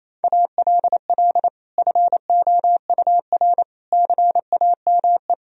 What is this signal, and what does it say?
Morse code, heard as a keyed tone, about 28 wpm, 700 Hz. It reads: ALL FOUR CAME
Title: Sentences from Top 1000 Words - Review Part 1 - 28wpm